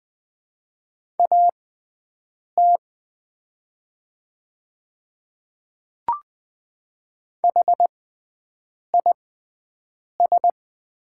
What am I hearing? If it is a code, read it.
Morse code, heard as a keyed tone, about 20 wpm, spaced as for 7 wpm, 700 Hz. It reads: AT HIS
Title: 6x Character Spacing - Top 200 Words - 20wpm